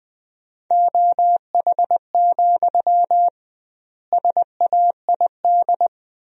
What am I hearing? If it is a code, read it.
Morse code, heard as a keyed tone, about 20 wpm, 700 Hz. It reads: OH, SAID
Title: 2x Word-Spacing - Sentences from Top 500 Words - 20wpm